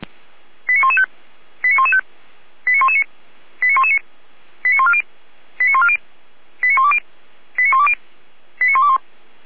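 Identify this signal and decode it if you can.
Five-tone selective call (ZVEI), heard as quick runs of five tones, tones of about 70 ms. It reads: 89115 89115 89119 89119 89140 89140 89120 89120 89121
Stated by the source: ZVEI five-tone selective call received by SDR.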